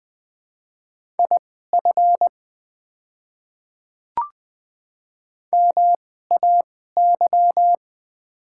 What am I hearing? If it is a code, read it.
Morse code, heard as a keyed tone, about 20 wpm, spaced as for 14 wpm, 690 Hz. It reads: IF MAY
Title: 2x Character Spacing - Top 100 Words - 20wpm